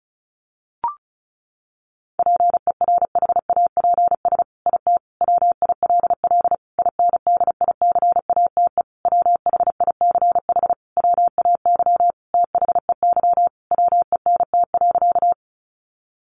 Morse code, heard as a keyed tone, about 34 wpm, 700 Hz. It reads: PERHAPS IT WILL INDICATE WHICH WAY THEY WENT.
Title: Sentences from Top 1300 Words 35wpm